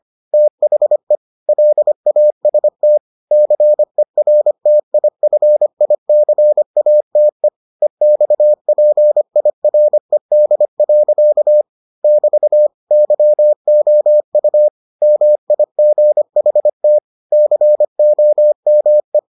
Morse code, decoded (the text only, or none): THE LAST CERTIFICATE EXPIRED. <BT> YOU MIGHT COME